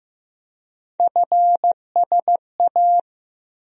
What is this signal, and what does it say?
Morse code, heard as a keyed tone, about 15 wpm, 700 Hz. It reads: FSA
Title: USA Call Sign Postfixes - 3 Letters 15wpm